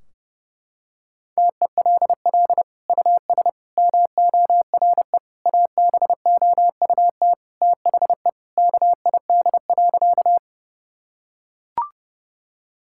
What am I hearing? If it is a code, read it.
Morse code, heard as a keyed tone, about 30 wpm, 700 Hz. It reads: TELL US MORE ABOUT THE KID.
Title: Sentences from Top 1600 Words 30wpm